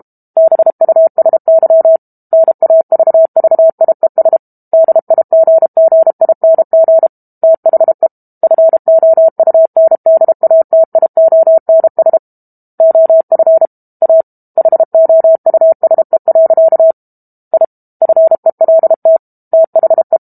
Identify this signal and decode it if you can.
Morse code, heard as a keyed tone, about 32 wpm, 650 Hz. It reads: BUSY NAVVIES DIGGING THE FOUNDATIONS OF A HOUSE. I FELT THE
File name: Book Two / 06-Chapter 6 - 32 WPM